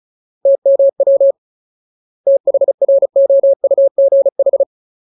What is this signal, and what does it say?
Morse code, heard as a keyed tone, about 34 wpm, 550 Hz. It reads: TMW THROUGH
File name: Ultimate Word List / 35WPM Ultimate